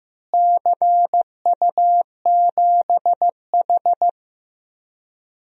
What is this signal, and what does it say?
Morse code, heard as a keyed tone, about 15 wpm, 700 Hz. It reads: CU7H
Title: DXCC-100 Call Signs - 4 Characters - Repeated 3x - 15wpm